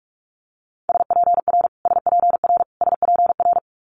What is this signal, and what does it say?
Morse code, heard as a keyed tone, about 46 wpm, 700 Hz. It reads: SPR SPR SPR
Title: Multi-letter Phonemes Repeated 3x 45wpm